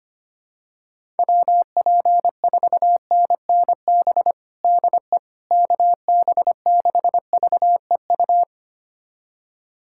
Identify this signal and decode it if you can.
Morse code, heard as a keyed tone, about 25 wpm, 700 Hz. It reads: WP4NNB DE KB6VEU
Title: USA Callsign DE Callsign - 6 Characters - 25wpm